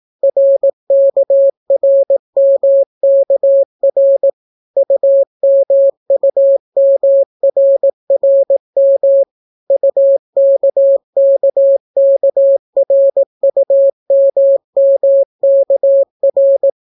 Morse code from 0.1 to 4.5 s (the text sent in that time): RKRMKR